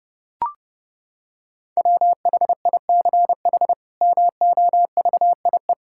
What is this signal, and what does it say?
Morse code, heard as a keyed tone, about 29 wpm, 700 Hz. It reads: WHICH MOVIE
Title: Sentences from Top 1000 Words - Review Part 5 - 30wpm